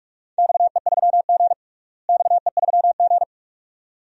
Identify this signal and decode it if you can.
Morse code, heard as a keyed tone, about 46 wpm, 700 Hz. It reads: XE3G XE3G
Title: DXCC-100 Call Signs - 4 Characters - Repeated 2x - 45wpm